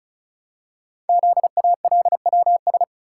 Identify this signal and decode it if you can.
Morse code, heard as a keyed tone, about 36 wpm, 700 Hz. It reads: ZALWS
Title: Code Groups - 5 Letters - 35wpm